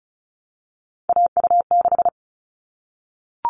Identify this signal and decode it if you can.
Morse code, heard as a keyed tone, about 34 wpm, 700 Hz. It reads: AU6
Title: All Cut Numbers - 3-Digit 35wpm